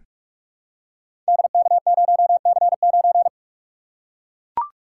Morse code, heard as a keyed tone, about 44 wpm, 700 Hz. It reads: DK0C9
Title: Code Groups - 5 Letter:Numbers - 45wpm